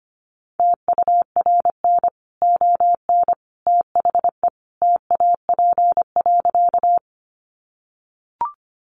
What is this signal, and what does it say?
Morse code, heard as a keyed tone, about 25 wpm, 700 Hz. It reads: TURN ON THE TAP.
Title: Sentences from Top 1400 Words 25wpm